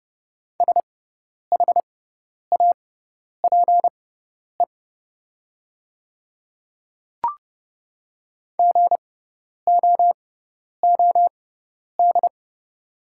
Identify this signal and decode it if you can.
Morse code, heard as a keyed tone, about 30 wpm, spaced as for 10 wpm, 690 Hz. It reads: SHAPE GOOD
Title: 6x Character Spacing - Top 500 Words - 30wpm